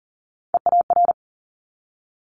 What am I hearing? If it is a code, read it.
Morse code, heard as a keyed tone, about 38 wpm, 700 Hz. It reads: EAR
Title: Multi-letter Phonemes 40wpm